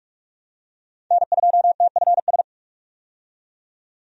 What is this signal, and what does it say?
Morse code, heard as a keyed tone, about 43 wpm, 700 Hz. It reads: N2TFS
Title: USA Call Signs - 5 Characters - 45wpm